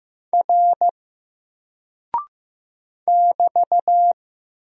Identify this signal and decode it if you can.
Morse code, heard as a keyed tone, about 15 wpm, 700 Hz. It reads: R <BT>
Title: Single Character (Letter:Number:Punctuation:Prosign) Alt 15wpm